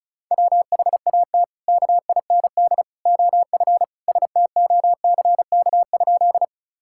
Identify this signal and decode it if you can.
Morse code, heard as a keyed tone, about 36 wpm, 700 Hz. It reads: WHAT KIND OF STOCK?
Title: Sentences from Top 1000 Words - Review Part 5 - 35wpm